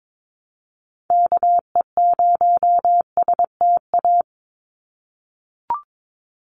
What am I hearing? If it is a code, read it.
Morse code, heard as a keyed tone, about 22 wpm, 700 Hz. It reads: KE0STA
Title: USA Call Signs 22wpm